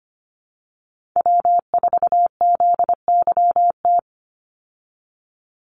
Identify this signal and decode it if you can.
Morse code, heard as a keyed tone, about 25 wpm, 700 Hz. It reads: W4ZYT
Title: Call Signs Repeated 3x - 25wpm